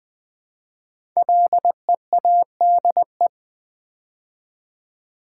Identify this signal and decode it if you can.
Morse code, heard as a keyed tone, about 20 wpm, 700 Hz. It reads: LEADE
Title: Build A Word - 5 Letters At A Time - 10-Letter Words - 20wpm